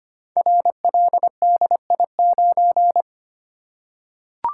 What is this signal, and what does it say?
Morse code, heard as a keyed tone, about 25 wpm, 690 Hz. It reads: RLDI9
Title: Code Groups - 5 Letter:Numbers - 25wpm